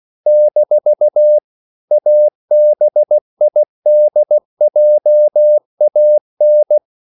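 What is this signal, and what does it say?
Morse code, heard as a keyed tone, about 16 wpm, 600 Hz. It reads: - ABIDJAN
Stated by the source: CW practice of national capitals at 16 words per minute, from ON6ZQ.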